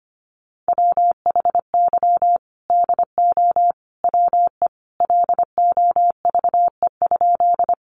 Morse code, heard as a keyed tone, about 25 wpm, 700 Hz. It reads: WHY DO WE LOVE?
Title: Sentences from Top 400 Words 25wpm